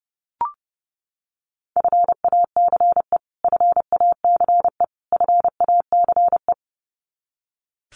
Morse code, heard as a keyed tone, about 30 wpm, 700 Hz. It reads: FACE FACE FACE
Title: Top 300 Words Repeated 30wpm